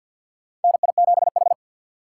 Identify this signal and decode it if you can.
Morse code, heard as a keyed tone, about 53 wpm, 700 Hz. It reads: NI7H
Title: USA Call Signs - 4 Characters - 3x Character Spacing - 50wpm